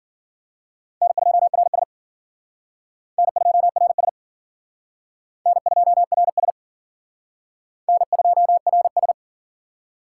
Morse code, heard as a keyed tone, about 51 wpm, 700 Hz. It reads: N2RS N2RS N2RS N2RS